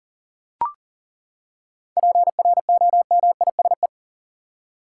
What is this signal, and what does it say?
Morse code, heard as a keyed tone, about 40 wpm, 700 Hz. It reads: PROMISE
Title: Seven Letter Words - 40wpm